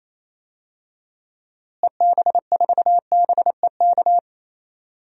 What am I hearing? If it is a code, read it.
Morse code, heard as a keyed tone, about 27 wpm, 700 Hz. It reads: EB4BEK
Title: DXCC-100 Call Signs - 6 Characters - Repeated 3x - 28wpm